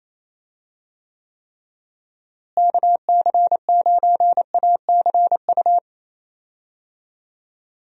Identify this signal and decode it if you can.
Morse code, heard as a keyed tone, about 28 wpm, 700 Hz. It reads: KC9ACU
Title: USA Call Signs - 6 Characters - Repeated 3x - 28wpm